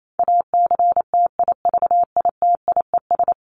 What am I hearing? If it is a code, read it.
Morse code, heard as a keyed tone, about 28 wpm, 700 Hz. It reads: ACTIVITIES
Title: ES:S Suffix - Top 100 Words with Base Words 28wpm